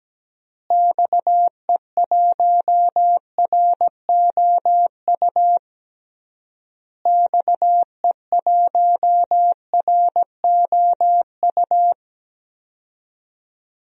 Morse code, heard as a keyed tone, about 17 wpm, 700 Hz. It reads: XE1ROU XE1ROU
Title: DXCC-100 Call Signs - 6 Characters - Repeated 3x - 17wpm